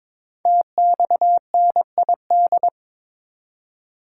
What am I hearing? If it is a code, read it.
Morse code, heard as a keyed tone, about 22 wpm, 700 Hz. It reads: TXNID
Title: Code Groups - 5 Letter:Numbers - 22wpm